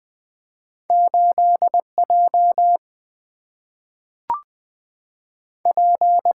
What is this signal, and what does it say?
Morse code, heard as a keyed tone, about 20 wpm, 700 Hz. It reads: 8 J P
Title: Single-Letter Single-Number 20wpm